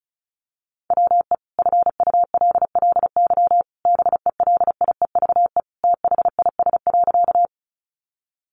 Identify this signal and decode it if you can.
Morse code, heard as a keyed tone, about 34 wpm, 700 Hz. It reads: WE FULLY BELIEVE THIS.
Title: Sentences from Top 1300 Words 35wpm